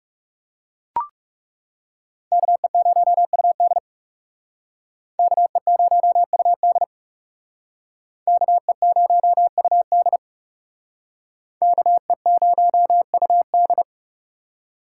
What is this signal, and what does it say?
Morse code, heard as a keyed tone, about 38 wpm, 700 Hz. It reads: KE0UD KE0UD KE0UD KE0UD